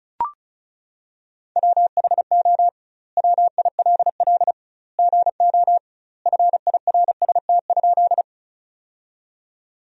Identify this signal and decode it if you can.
Morse code, heard as a keyed tone, about 36 wpm, 700 Hz. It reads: WHO WILL GO FIRST?
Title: 2x Word-Spacing - Sentences from Top 100 Words - 35wpm